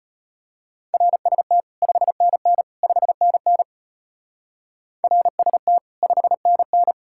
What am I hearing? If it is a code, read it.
Morse code, heard as a keyed tone, about 30 wpm, 700 Hz. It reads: RST 5NN 5NN RST 5NN
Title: Speed-Racing Common QSO Phrases - 25wpm